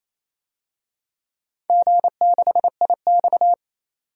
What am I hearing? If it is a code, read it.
Morse code, heard as a keyed tone, about 28 wpm, 700 Hz. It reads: G6IX
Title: DXCC-100 Call Signs - 4 Characters - 28wpm